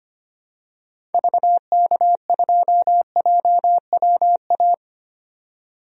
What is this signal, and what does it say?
Morse code, heard as a keyed tone, about 25 wpm, 690 Hz. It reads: VK2JWA
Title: DXCC-100 Call Signs - 6 Characters - 25wpm